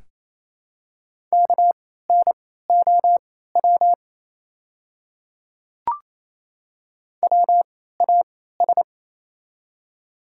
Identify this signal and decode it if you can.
Morse code, heard as a keyed tone, about 28 wpm, spaced as for 16 wpm, 700 Hz. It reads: KNOW WAS
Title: 3x Character Spacing - Top 200 Words - 28wpm